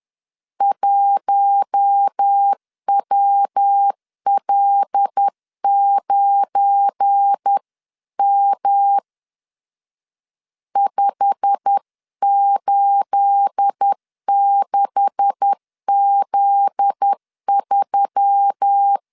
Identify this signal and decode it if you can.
Morse code, heard as a keyed tone, about 10 wpm, 800 Hz. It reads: 1WL9M 586Z3
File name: CW / DET / 1 9 1500C S A 10wpm 4511